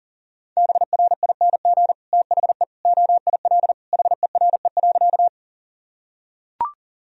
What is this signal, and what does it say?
Morse code, heard as a keyed tone, about 40 wpm, 700 Hz. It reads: BRING THE OIL HERE.
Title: Sentences from Top 1000 Words - Review Part 4 - 40wpm